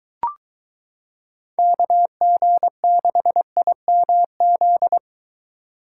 Morse code, heard as a keyed tone, about 23 wpm, 700 Hz. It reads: KG6IMZ